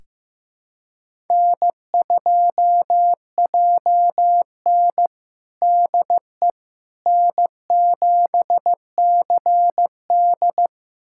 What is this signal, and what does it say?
Morse code, heard as a keyed tone, about 15 wpm, 700 Hz. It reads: N2JN DE N7CD
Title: USA Callsign DE Callsign - 4 Characters - 15wpm